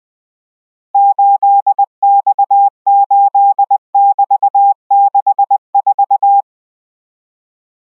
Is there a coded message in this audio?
Yes: Morse code from 0.9 to 6.4 s, heard as a keyed tone, about 20 wpm, 800 Hz. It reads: 8X8<BT>64